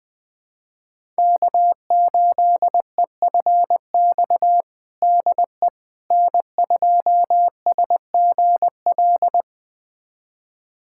Morse code, heard as a keyed tone, about 19 wpm, 700 Hz. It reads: K8EFX DE N2SGL
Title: USA Callsign DE Callsign - 5 Characters - 20wpm